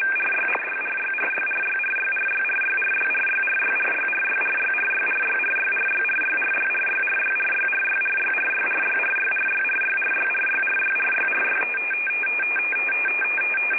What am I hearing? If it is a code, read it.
RTTY, about 45.45 baud. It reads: RYRYRYRYRYRYRYRYRYRYRYRYRYRYRYRYRYRYRYRYRYRYRYRYRYRYRYRYRYRYRYRYRYRYRY
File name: rtty / websdr recording 14089.1kHz